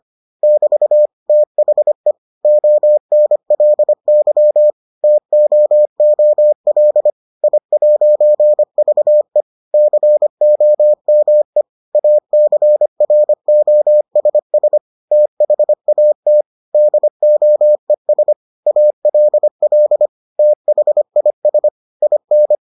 Morse code, heard as a keyed tone, about 25 wpm, 600 Hz. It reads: <BT> THE ONLY TOOL I'VE COME ACROSS THAT DOES ALL THIS IN